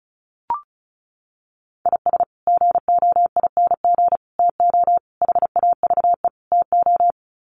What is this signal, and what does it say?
Morse code, heard as a keyed tone, about 34 wpm, 700 Hz. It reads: IS GOING TO HAVE TO
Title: Sets of 5 Words 35wpm